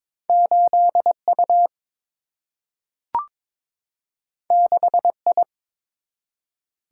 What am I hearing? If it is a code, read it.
Morse code, heard as a keyed tone, about 21 wpm, 700 Hz. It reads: 8 U 6 I
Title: Single-Letter Single-Number 22wpm